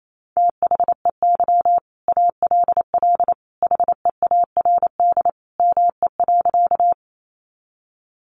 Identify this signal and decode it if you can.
Morse code, heard as a keyed tone, about 28 wpm, 700 Hz. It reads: THEY ALL HEARD ME.
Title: Sentences from Top 400 Words 28wpm